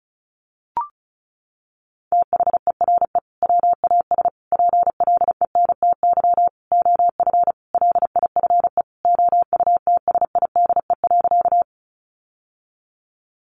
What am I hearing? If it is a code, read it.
Morse code, heard as a keyed tone, about 34 wpm, 700 Hz. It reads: THERE WAS PLENTY OF LIFE OUTSIDE.
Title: Sentences from Top 1300 Words 35wpm